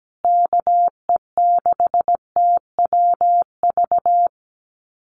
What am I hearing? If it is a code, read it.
Morse code, heard as a keyed tone, about 17 wpm, 700 Hz. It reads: KE6TWV
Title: USA Call Signs Repeated 2x 17wpm